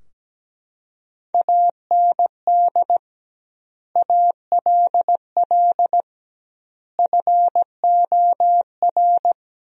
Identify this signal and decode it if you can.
Morse code, heard as a keyed tone, about 17 wpm, 700 Hz. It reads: AND ALL FOR
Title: 2x Word-Spacing - Sentences from Top 100 Words - 17wpm